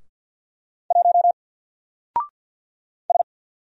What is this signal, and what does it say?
Morse code, heard as a keyed tone, about 50 wpm, 700 Hz. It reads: 1 S
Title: Single Character (Letter:Number:Punctuation:Prosign) 50wpm